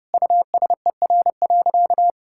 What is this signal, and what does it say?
Morse code, heard as a keyed tone, about 32 wpm, 700 Hz. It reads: USER.
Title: Sentences from Top 1000 Words - Review Part 5 - 30wpm